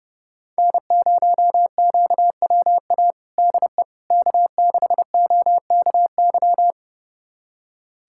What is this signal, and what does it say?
Morse code, heard as a keyed tone, about 30 wpm, 700 Hz. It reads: N0QWA DE K6OKY
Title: USA Callsign DE Callsign - 5 Characters - 30wpm